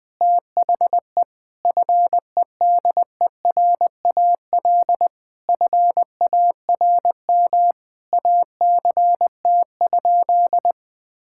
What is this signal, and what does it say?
Morse code, heard as a keyed tone, about 20 wpm, 700 Hz. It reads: THE FEDERAL FARM ACT?
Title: Sentences from Top 1000 Words - Review Part 5 - 20wpm